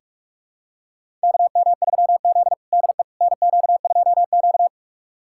Morse code, heard as a keyed tone, about 46 wpm, 700 Hz. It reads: KM3Z DE NQ2Q
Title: USA Callsign DE Callsign - 4 Characters - 45wpm